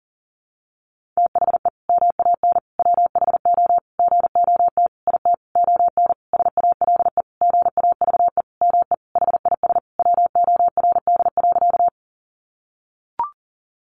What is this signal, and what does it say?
Morse code, heard as a keyed tone, about 40 wpm, 700 Hz. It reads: THE MAN WHO GOT IT ON SALE GAVE ME HIS WORD.
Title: Sentences from Top 1400 Words 40wpm